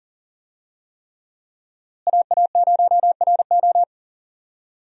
Morse code, heard as a keyed tone, about 39 wpm, 700 Hz. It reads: AA0RO
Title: USA Call Signs - 5 Characters - Repeated 3x - 40wpm